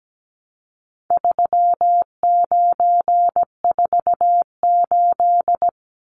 Morse code, heard as a keyed tone, about 17 wpm, 700 Hz. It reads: 3948
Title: Numbers - 4-Digit 17wpm